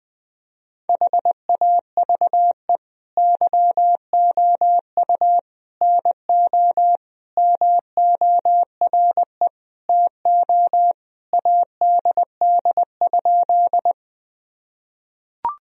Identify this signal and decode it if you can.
Morse code, heard as a keyed tone, about 20 wpm, 700 Hz. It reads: HAVE YOU NO MORE TO ADD?
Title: Sentences from Top 1000 Words - Review Part 1 - 20wpm